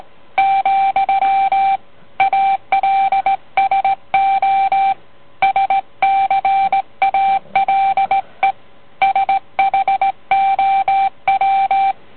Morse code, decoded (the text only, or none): , ALSO SCALE SHOW